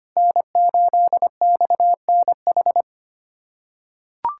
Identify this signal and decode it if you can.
Morse code, heard as a keyed tone, about 25 wpm, 700 Hz. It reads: N8XNH
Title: USA Callsign DE Callsign - 5 Characters - 25wpm